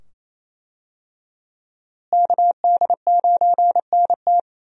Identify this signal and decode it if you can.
Morse code, heard as a keyed tone, about 27 wpm, 700 Hz. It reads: KD9NT
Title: USA Call Signs - 5 Characters - Repeated 2x - 28wpm